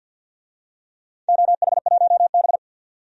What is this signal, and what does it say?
Morse code, heard as a keyed tone, about 46 wpm, 700 Hz. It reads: OH1B
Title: DXCC-100 Call Signs - 4 Characters - 50wpm